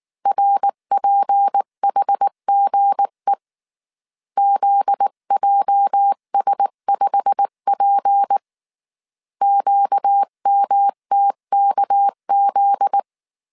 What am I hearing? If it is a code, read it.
Morse code, heard as a keyed tone, about 19 wpm, 800 Hz. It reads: RPHGE ZJS5P QMTKZ
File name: CW / DET / 1 9 1500C S B 18wpm 2551